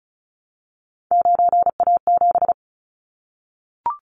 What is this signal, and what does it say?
Morse code, heard as a keyed tone, about 36 wpm, 700 Hz. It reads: 9A7
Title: Callsign Prefixes for Top 100 DXCC - 35wpm